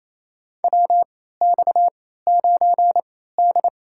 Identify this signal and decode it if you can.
Morse code, heard as a keyed tone, about 28 wpm, spaced as for 16 wpm, 700 Hz. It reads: WX9D
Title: USA Call Signs - 4 Characters - 3x Character Spacing - 28wpm